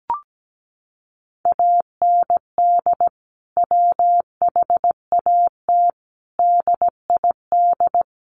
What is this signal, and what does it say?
Morse code, heard as a keyed tone, about 17 wpm, 700 Hz. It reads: AND WHAT DID
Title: Sentences from Top 300 Words 17wpm